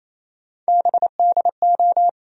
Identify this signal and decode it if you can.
Morse code, heard as a keyed tone, about 28 wpm, 700 Hz. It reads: BDO
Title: USA Call Sign Postfixes - 3 Letters 28wpm